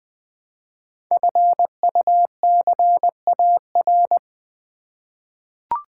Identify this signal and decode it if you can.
Morse code, heard as a keyed tone, about 20 wpm, 700 Hz. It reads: FUCAR
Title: Code Groups - 5 Letter:Numbers - 20wpm